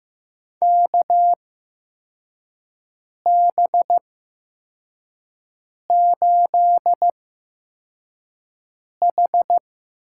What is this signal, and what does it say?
Morse code, heard as a keyed tone, about 15 wpm, spaced as for 4 wpm, 700 Hz. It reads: KB8H
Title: USA Call Signs - 8x Character Spacing - 15wpm